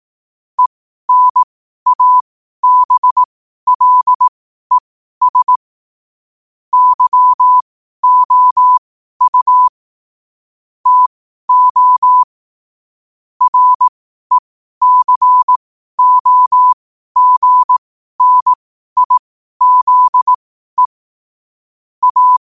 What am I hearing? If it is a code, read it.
Morse code, heard as a keyed tone, about 18 wpm, spaced as for 12 wpm, 1000 Hz. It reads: ENABLES YOU TO RECOGNIZE A
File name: qotdfiles / 12wpm